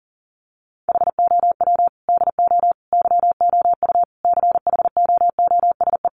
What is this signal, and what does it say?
Morse code, heard as a keyed tone, about 40 wpm, 700 Hz. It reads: HOW DO YOU CHOOSE
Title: Sentences from Top 700 Words 40wpm